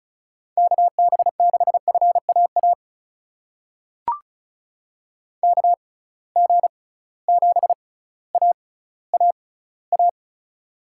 Morse code, heard as a keyed tone, about 36 wpm, spaced as for 18 wpm, 700 Hz. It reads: KB6FAA KG7AAA
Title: USA Call Signs - 6 Characters - 6x Character Spacing - 35wpm